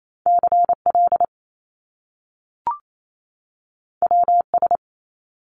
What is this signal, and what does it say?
Morse code, heard as a keyed tone, about 28 wpm, 700 Hz. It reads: CL WS
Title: Top Two Letter Combinations in Words 28wpm